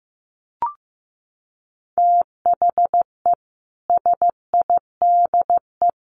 Morse code, heard as a keyed tone, about 15 wpm, 700 Hz. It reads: THE SIDE
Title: Sets of 5 Words 15wpm